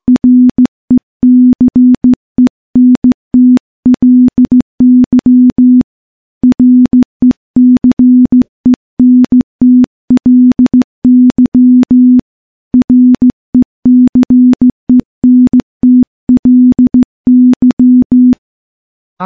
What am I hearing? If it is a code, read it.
Morse code, heard as a keyed tone, about 15 wpm, 250 Hz. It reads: RECENTLY RECENTLY RECENTLY
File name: morse / morse-training-15WPM